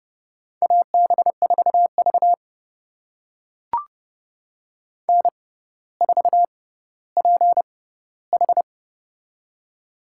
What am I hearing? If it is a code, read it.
Morse code, heard as a keyed tone, about 30 wpm, spaced as for 15 wpm, 700 Hz. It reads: AB4V N4PH
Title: USA Call Signs - 4 Characters - 6x Character Spacing - 30wpm